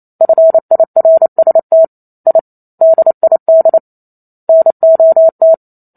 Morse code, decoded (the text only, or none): FIRST I DID NOT